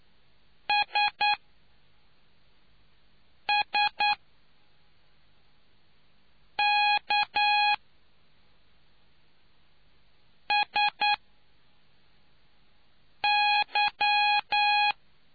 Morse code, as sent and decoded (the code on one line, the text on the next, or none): ... ... -.- ... -.--
SSKSY